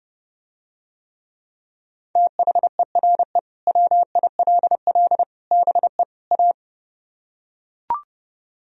Morse code, heard as a keyed tone, about 30 wpm, 700 Hz. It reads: THERE WILL BE A